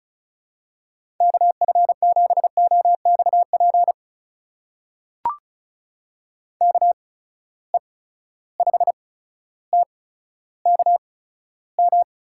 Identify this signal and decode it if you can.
Morse code, heard as a keyed tone, about 34 wpm, spaced as for 15 wpm, 700 Hz. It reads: KF7OXP KE5TKM
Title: USA Call Signs - 6 Characters - 8x Character Spacing - 35wpm